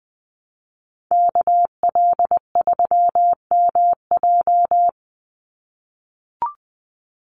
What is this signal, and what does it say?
Morse code, heard as a keyed tone, about 20 wpm, 700 Hz. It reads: KL3MJ
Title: USA Call Signs 20wpm